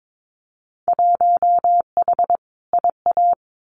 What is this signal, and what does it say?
Morse code, heard as a keyed tone, about 22 wpm, 700 Hz. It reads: 1H IA
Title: Winter Field Day Exchanges 22wpm